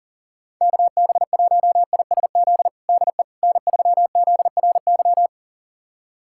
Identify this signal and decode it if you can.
Morse code, heard as a keyed tone, about 40 wpm, 700 Hz. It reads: KB1ISZ DE N3ZRY
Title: USA Callsign DE Callsign 40wpm